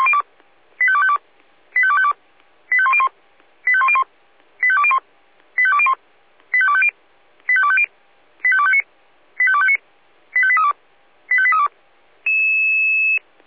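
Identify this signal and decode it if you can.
Five-tone selective call (ZVEI), heard as quick runs of five tones, tones of about 70 ms. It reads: 85252 85252 85101 85101 85201 85201 85250 85250 85259 85259 85832 85832
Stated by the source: ZVEI five-tone selective call received with an SDR.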